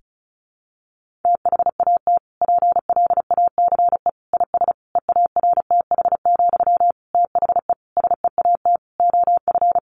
Morse code, heard as a keyed tone, about 34 wpm, 700 Hz. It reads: THAT PLACE IS EARTH, THE SEAT OF
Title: Sentences from Top 1500 Words 35wpm